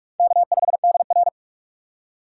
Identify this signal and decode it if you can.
Morse code, heard as a keyed tone, about 44 wpm, 700 Hz. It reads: K5DR
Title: USA Call Signs - 8x Character Spacing - 45wpm